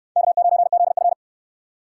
Morse code, heard as a keyed tone, about 60 wpm, 700 Hz. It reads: K9ZF